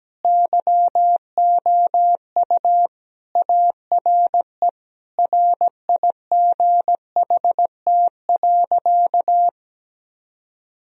Morse code, heard as a keyed tone, about 17 wpm, 700 Hz. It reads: YOU ARE RIGHT.
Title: Sentences from Top 1000 Words - Review Part 4 - 17wpm